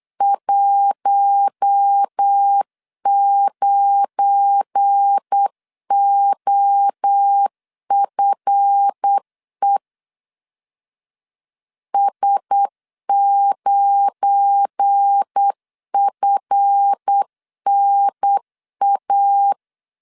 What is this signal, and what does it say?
Morse code, heard as a keyed tone, about 8 wpm, 800 Hz. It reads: 19OFE S9FNA
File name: LEZIONI / 7 GF9Z / 7 GF9 8wpm 3545